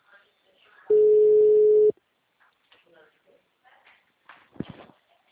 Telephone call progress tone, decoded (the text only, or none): ringback tone